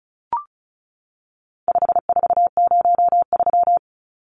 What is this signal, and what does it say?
Morse code, heard as a keyed tone, about 34 wpm, 700 Hz. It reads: 5403
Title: Numbers - 4-Digit 35wpm